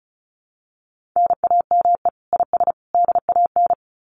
Morse code, heard as a keyed tone, about 34 wpm, 700 Hz. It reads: NAME IS DAN
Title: Common QSO Phrases 35wpm